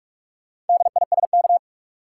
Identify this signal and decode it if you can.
Morse code, heard as a keyed tone, about 43 wpm, 700 Hz. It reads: DISK
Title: Sentences from Top 1000 Words - Review Part 6 - New Word Drill - 45wpm